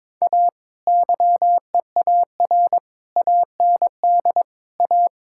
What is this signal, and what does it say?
Morse code, heard as a keyed tone, about 22 wpm, 700 Hz. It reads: A YEAR AND A
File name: practice / 22wpm / Build A Phrase - 5-Word Phrases - 22wpm